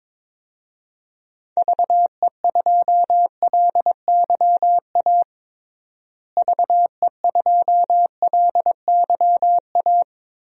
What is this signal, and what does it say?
Morse code, heard as a keyed tone, about 22 wpm, 700 Hz. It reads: VE2LYA VE2LYA
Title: DXCC-100 Call Signs - 6 Characters - Repeated 2x - 22wpm